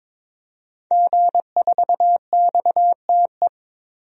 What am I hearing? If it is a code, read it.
Morse code, heard as a keyed tone, about 21 wpm, 700 Hz. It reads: G4XTE
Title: DXCC-100 Call Signs - 5 Characters - 22wpm